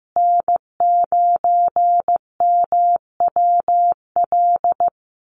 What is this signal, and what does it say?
Morse code, heard as a keyed tone, about 15 wpm, 700 Hz. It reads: N9MWL
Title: USA Call Signs Repeated 2x 15wpm